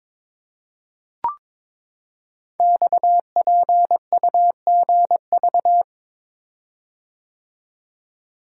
Morse code, heard as a keyed tone, about 22 wpm, 700 Hz. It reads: XPUGV
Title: Code Groups - 5 Letters - 22wpm